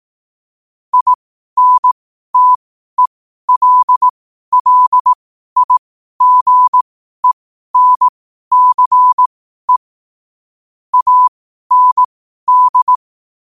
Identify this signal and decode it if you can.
Morse code, heard as a keyed tone, about 18 wpm, spaced as for 12 wpm, 1000 Hz. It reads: INTELLIGENCE AND
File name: 12wpm